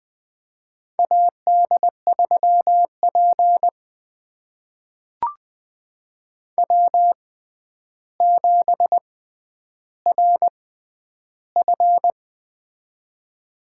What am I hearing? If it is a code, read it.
Morse code, heard as a keyed tone, about 20 wpm, spaced as for 10 wpm, 700 Hz. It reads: AD3P W7RF
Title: USA Call Signs - 4 Characters - 6x Character Spacing - 20wpm